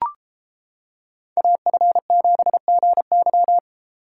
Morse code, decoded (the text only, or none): AF7GY